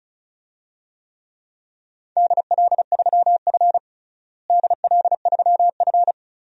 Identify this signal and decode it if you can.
Morse code, heard as a keyed tone, about 36 wpm, 700 Hz. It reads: DL3F DL3F
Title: DXCC-100 Call Signs - 4 Characters - Repeated 2x - 35wpm